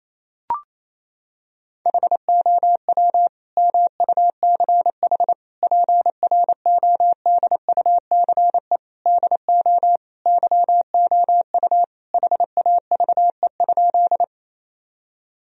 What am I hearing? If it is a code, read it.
Morse code, heard as a keyed tone, about 28 wpm, 700 Hz. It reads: HOW MUCH PRODUCE DO YOU HAVE?
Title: Sentences from Top 1000 Words - Review Part 3 - 28wpm